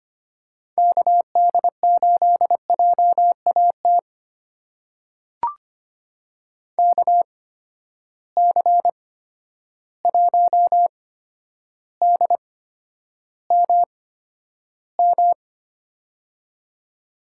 Morse code, heard as a keyed tone, about 25 wpm, spaced as for 11 wpm, 700 Hz. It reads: KD8JAT KC1DMM
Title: USA Call Signs - 6 Characters - 8x Character Spacing - 25wpm